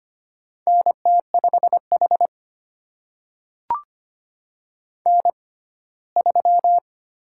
Morse code, heard as a keyed tone, about 25 wpm, spaced as for 17 wpm, 700 Hz. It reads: NT5H N3
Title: USA Call Signs - 4 Characters - 6x Character Spacing - 25wpm